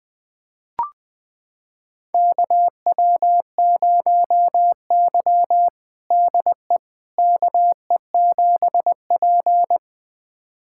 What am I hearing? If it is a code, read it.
Morse code, heard as a keyed tone, about 21 wpm, 700 Hz. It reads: KW0Y DE KE7P
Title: USA Callsign DE Callsign - 4 Characters - 20wpm